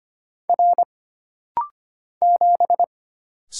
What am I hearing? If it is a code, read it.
Morse code, heard as a keyed tone, about 25 wpm, 700 Hz. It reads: R 7
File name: Single Character (Letter:Number:Punctuation:Prosign) - Rapid-Fire - 25wpm